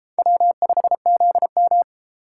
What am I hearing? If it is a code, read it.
Morse code, heard as a keyed tone, about 33 wpm, 700 Hz. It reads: W5ZM